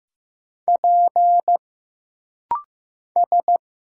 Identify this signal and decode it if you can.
Morse code, heard as a keyed tone, about 15 wpm, 700 Hz. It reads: P S
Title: Single Letter-Number Alt - Rapid-Fire 15wpm